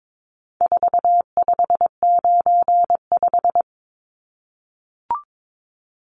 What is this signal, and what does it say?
Morse code, heard as a keyed tone, about 22 wpm, 700 Hz. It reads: 4595
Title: Numbers - 4-Digit 22wpm